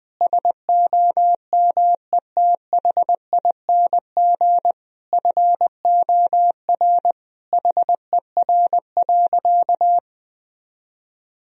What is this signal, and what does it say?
Morse code, heard as a keyed tone, about 21 wpm, 700 Hz. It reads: SOMETHING FOR HER.
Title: Sentences from Top 1000 Words - Review Part 1 - 20wpm